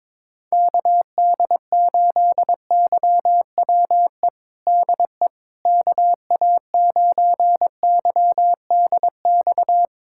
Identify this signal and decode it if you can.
Morse code, heard as a keyed tone, about 22 wpm, 700 Hz. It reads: KD8YWE DE KA9YDX
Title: USA Callsign DE Callsign 22wpm